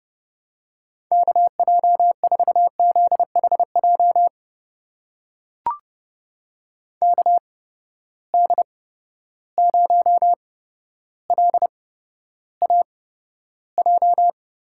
Morse code, heard as a keyed tone, about 32 wpm, spaced as for 13 wpm, 700 Hz. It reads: KJ4ZHJ KD0LAJ
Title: USA Call Signs - 6 Characters - 8x Character Spacing - 30wpm